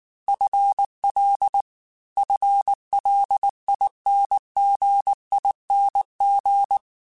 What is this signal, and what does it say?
Morse code, heard as a keyed tone, about 19 wpm, 800 Hz. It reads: FL FLINGING
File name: FL / FL mp3s / FL Prefixes 3x2 Twice @18